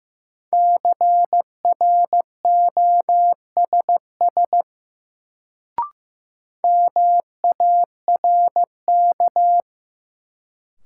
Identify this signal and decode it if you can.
Morse code, heard as a keyed tone, about 15 wpm, 700 Hz. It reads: CROSS MARK
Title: Sentences from Top 300 Words - New Word Drill - 15wpm